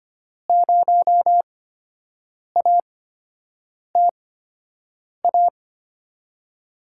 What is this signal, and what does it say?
Morse code, heard as a keyed tone, about 25 wpm, spaced as for 7 wpm, 700 Hz. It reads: 0ATA
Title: USA Call Signs - 5 Characters - 8x Character Spacing - 25wpm